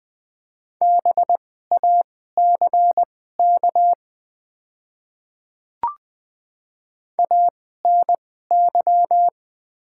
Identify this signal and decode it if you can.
Morse code, heard as a keyed tone, about 20 wpm, spaced as for 14 wpm, 700 Hz. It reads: BACK ANY
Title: 2x Character Spacing - Top 200 Words - 20wpm